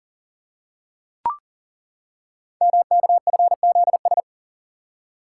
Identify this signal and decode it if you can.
Morse code, heard as a keyed tone, about 40 wpm, 690 Hz. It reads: MKFZS
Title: Code Groups - 5 Letters - 40wpm